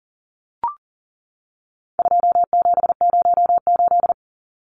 Morse code, heard as a keyed tone, about 37 wpm, 700 Hz. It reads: 2708
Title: Numbers - 4-Digit 40wpm